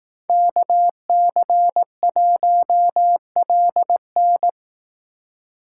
Morse code, heard as a keyed tone, about 18 wpm, 700 Hz. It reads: KC1LN
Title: Speed Racing - USA 2x2 Call Signs 15wpm